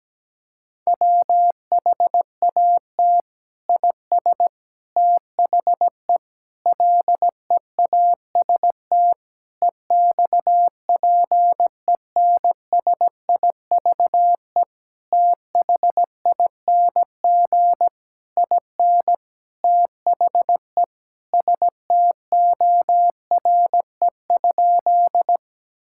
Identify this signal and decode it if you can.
Morse code, heard as a keyed tone, about 17 wpm, 700 Hz. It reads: WHAT IS THE LEAST EXPENSIVE THING IN THE STORE?
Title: Sentences from Top 1000 Words - Review Part 6 - 17wpm